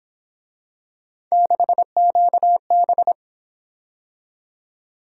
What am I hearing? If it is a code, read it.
Morse code, heard as a keyed tone, about 26 wpm, 700 Hz. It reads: 6QB
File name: Speed-Racing - Three Letter-Number Combinations - 22wpm